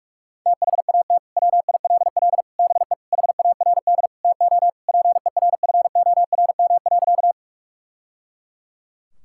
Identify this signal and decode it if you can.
Morse code, heard as a keyed tone, about 41 wpm, 700 Hz. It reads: THAT WILL BE HARD TO PERFORM.
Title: Sentences from Top 1000 Words - Review Part 5 - 45wpm